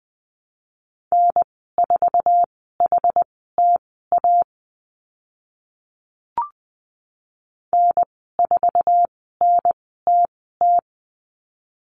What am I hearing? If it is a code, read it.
Morse code, heard as a keyed tone, about 20 wpm, spaced as for 14 wpm, 700 Hz. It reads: N4HTA N4NTT
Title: 062 Lesson 11g - Letter h - Call Signs 20wpm